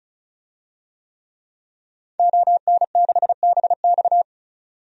Morse code, heard as a keyed tone, about 36 wpm, 700 Hz. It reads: ON6BX
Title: DXCC-100 Call Signs - 5 Characters - 35wpm